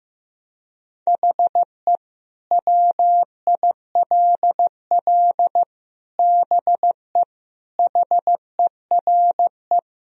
Morse code, heard as a keyed tone, about 15 wpm, 700 Hz. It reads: HE WILL BE HERE
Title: Sentences from Top 1000 Words - Review Part 3 - 15wpm